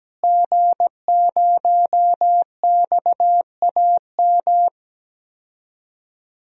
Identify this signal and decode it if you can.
Morse code, heard as a keyed tone, about 17 wpm, 700 Hz. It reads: G0XAM
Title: DXCC-100 Call Signs - 5 Characters - Repeated 3x - 17wpm